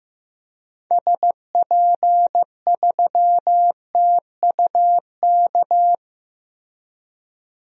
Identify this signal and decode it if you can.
Morse code, heard as a keyed tone, about 15 wpm, 700 Hz. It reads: SP3TUK
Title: DXCC-100 Call Signs - 6 Characters - Repeated 3x - 15wpm